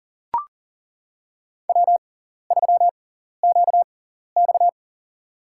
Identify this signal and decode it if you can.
Morse code, heard as a keyed tone, about 40 wpm, spaced as for 14 wpm, 690 Hz. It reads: W3QX
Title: USA Call Signs - 6x Character Spacing - 40wpm